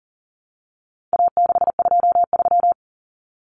Morse code, heard as a keyed tone, about 40 wpm, 700 Hz. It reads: A623
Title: Common Cut Numbers - 4-Digit 40wpm